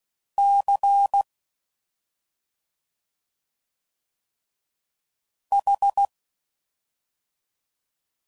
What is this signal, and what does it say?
Morse code, heard as a keyed tone, about 16 wpm, 800 Hz. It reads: C H